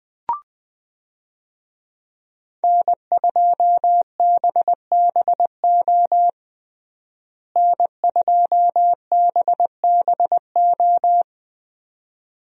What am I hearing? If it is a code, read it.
Morse code, heard as a keyed tone, about 20 wpm, 700 Hz. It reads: N2BBO N2BBO
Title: USA Call Signs - 5 Characters - Repeated 2x - 20wpm